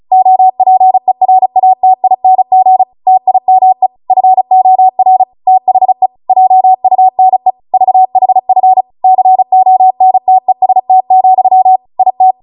Morse code, heard as a keyed tone, about 36 wpm, 750 Hz. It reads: OPERATING TIME FOR THE JUNE VHF CONTEST, IT
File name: Morse / 35WPM